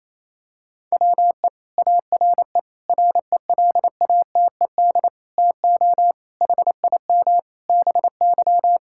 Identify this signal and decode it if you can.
Morse code, heard as a keyed tone, about 28 wpm, 700 Hz. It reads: WE ARE RELATED TO HIM BY
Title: Sentences from Top 1000 Words - Review Part 5 - 28wpm